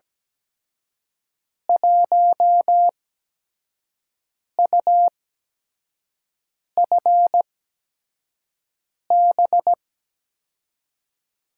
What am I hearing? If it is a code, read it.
Morse code, heard as a keyed tone, about 17 wpm, spaced as for 5 wpm, 700 Hz. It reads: 1UFB
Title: USA Call Signs - 5 Characters - 8x Character Spacing - 17wpm